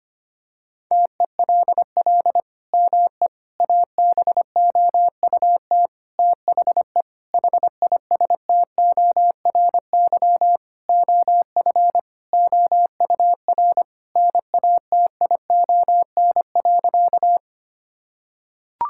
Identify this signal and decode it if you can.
Morse code, heard as a keyed tone, about 25 wpm, 700 Hz. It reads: TELL ME ABOUT THE HISTORY OF OUR NATION.
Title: Sentences from Top 1000 Words - Review Part 6 - 25wpm